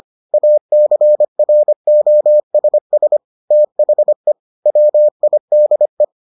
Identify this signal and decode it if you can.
Morse code, heard as a keyed tone, about 25 wpm, 600 Hz. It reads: ACROSS THE WIDE